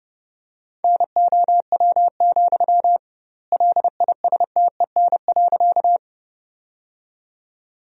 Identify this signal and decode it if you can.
Morse code, heard as a keyed tone, about 32 wpm, 700 Hz. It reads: NOW, LISTEN.
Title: 2x Word-Spacing - Sentences from Top 400 Words - 30wpm